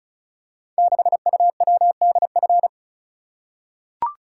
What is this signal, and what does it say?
Morse code, heard as a keyed tone, about 36 wpm, 700 Hz. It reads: 6UWDF
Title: Code Groups - 5 Letter:Numbers - 35wpm